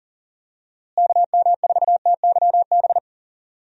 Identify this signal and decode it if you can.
Morse code, heard as a keyed tone, about 40 wpm, 700 Hz. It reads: KM4TYB
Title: USA Call Signs - 6 Characters - Repeated 2x - 40wpm